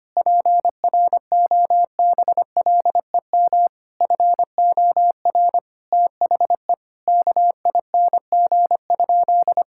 Morse code, heard as a keyed tone, about 25 wpm, 700 Hz. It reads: PROBLEM FOR THE KING?
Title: Sentences from Top 1000 Words - Review Part 5 - 25wpm